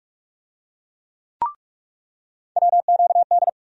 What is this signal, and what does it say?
Morse code, heard as a keyed tone, about 46 wpm, 700 Hz. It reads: WQD
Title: USA Call Sign Postfixes - 3 Letters 45wpm